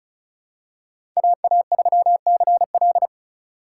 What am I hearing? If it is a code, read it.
Morse code, heard as a keyed tone, about 36 wpm, 700 Hz. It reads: AA3CL
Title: USA Call Signs - 5 Characters - 8x Character Spacing - 35wpm